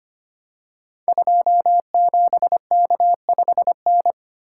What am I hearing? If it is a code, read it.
Morse code, heard as a keyed tone, about 25 wpm, 700 Hz. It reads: 27K5N
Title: Code Groups - 5 Letter:Numbers - 25wpm